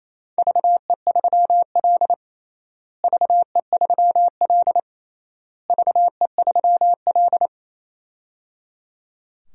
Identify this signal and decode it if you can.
Morse code, heard as a keyed tone, about 27 wpm, 700 Hz. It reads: VE3L VE3L VE3L
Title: DXCC-100 Call Signs - 4 Characters - Repeated 3x - 28wpm